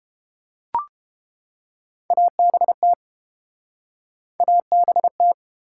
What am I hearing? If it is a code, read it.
Morse code, heard as a keyed tone, about 30 wpm, 700 Hz. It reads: ABT ABT